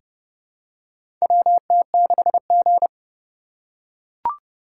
Morse code, heard as a keyed tone, about 30 wpm, 700 Hz. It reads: WT6G
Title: USA Call Signs - 4 Characters - 4x Character Spacing - 30wpm